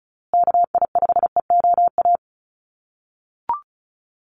Morse code, heard as a keyed tone, about 34 wpm, 700 Hz. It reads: KI5EOA
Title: USA Call Signs Repeated 3x 35wpm